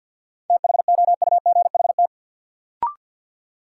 Morse code, heard as a keyed tone, about 53 wpm, 700 Hz. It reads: THOUGHT
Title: Sentences from Top 300 Words - New Word Drill - 50wpm